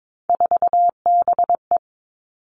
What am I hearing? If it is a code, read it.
Morse code, heard as a keyed tone, about 22 wpm, 700 Hz. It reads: 4BE
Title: Three Challenging Letter-Number Combinations 22wpm